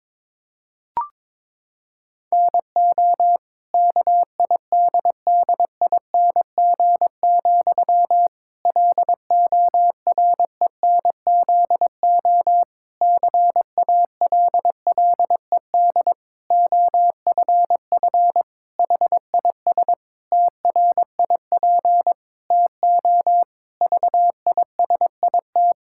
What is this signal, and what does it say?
Morse code, heard as a keyed tone, about 22 wpm, 700 Hz. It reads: NO KIDDING, LORENZO CALLED OFF HIS TRIP TO VISIT
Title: Pangram Challenge v2 - 22wpm